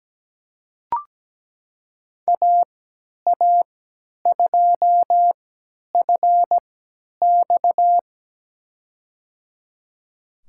Morse code, heard as a keyed tone, about 17 wpm, spaced as for 10 wpm, 700 Hz. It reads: AA2FX
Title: USA Call Signs - 5 Characters - 3x Character Spacing - 17wpm